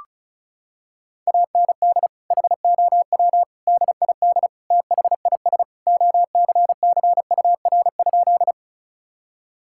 Morse code, heard as a keyed tone, about 36 wpm, 700 Hz. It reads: AND HOW DID THIS OCCUR?
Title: Sentences from Top 1000 Words - Review Part 6 - 35wpm